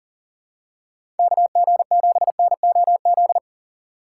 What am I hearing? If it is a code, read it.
Morse code, heard as a keyed tone, about 37 wpm, 700 Hz. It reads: KG7NOZ
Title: USA Call Signs - 6 Characters - 3x Character Spacing - 40wpm